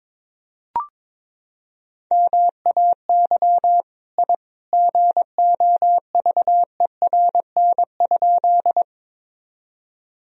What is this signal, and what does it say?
Morse code, heard as a keyed tone, about 22 wpm, 700 Hz. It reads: MAY I GOVERN?
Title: Sentences from Top 1000 Words - Review Part 2 - 22wpm